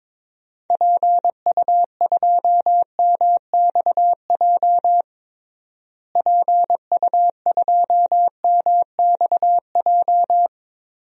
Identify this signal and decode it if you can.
Morse code, heard as a keyed tone, about 21 wpm, 700 Hz. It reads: PU2MXJ PU2MXJ
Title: DXCC-100 Call Signs - 6 Characters - Repeated 2x - 22wpm